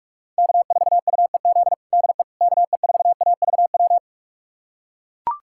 Morse code, heard as a keyed tone, about 43 wpm, 700 Hz. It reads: K4UEZ DE KE4AVW
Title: USA Callsign DE Callsign 45wpm